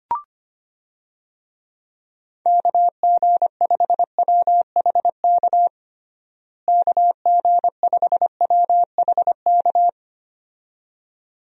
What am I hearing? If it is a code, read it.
Morse code, heard as a keyed tone, about 25 wpm, 700 Hz. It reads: KG5WHK KG5WHK
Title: USA Call Signs - 6 Characters - Repeated 2x - 25wpm